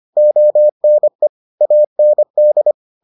Morse code, heard as a keyed tone, about 25 wpm, 600 Hz. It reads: ONE AND